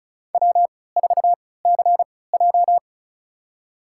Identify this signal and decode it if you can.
Morse code, heard as a keyed tone, about 36 wpm, spaced as for 20 wpm, 700 Hz. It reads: W4CJ
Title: USA Call Signs - 3x Character Spacing - 35wpm